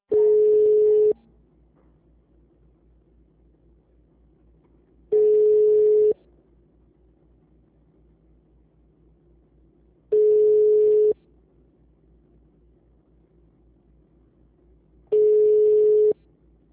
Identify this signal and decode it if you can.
Ringback tone (Europe, 425 Hz).